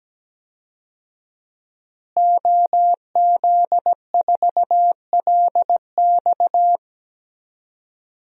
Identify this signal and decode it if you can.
Morse code, heard as a keyed tone, about 17 wpm, 700 Hz. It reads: OZ4LX
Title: DXCC-100 Call Signs - 5 Characters - Repeated 2x - 17wpm